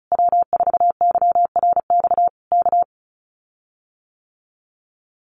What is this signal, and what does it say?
Morse code, heard as a keyed tone, about 34 wpm, 700 Hz. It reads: W4YRX K
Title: Common QSO Phrases Encore 35wpm